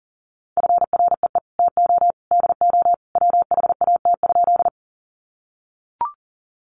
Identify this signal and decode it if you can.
Morse code, heard as a keyed tone, about 40 wpm, 700 Hz. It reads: FREE TO DO WHAT?
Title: Sentences from Top 500 Words 40wpm